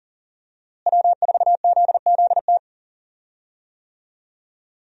Morse code, heard as a keyed tone, about 37 wpm, 700 Hz. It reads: W4ZZT
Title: USA Call Signs - 5 Characters - Repeated 3x - 40wpm